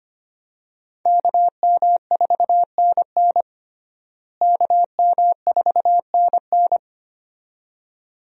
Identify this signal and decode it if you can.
Morse code, heard as a keyed tone, about 25 wpm, 700 Hz. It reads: KM4NN KM4NN
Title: USA Call Signs - 5 Characters - Repeated 2x - 25wpm